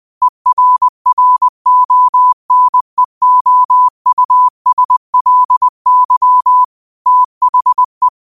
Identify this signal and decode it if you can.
Morse code, heard as a keyed tone, about 20 wpm, 1000 Hz. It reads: ERRONEOUSLY THE
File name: 20wpm